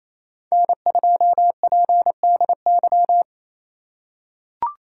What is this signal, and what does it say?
Morse code, heard as a keyed tone, about 28 wpm, 700 Hz. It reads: N2PDY
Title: USA Call Signs - 2x Character Spacing - 28wpm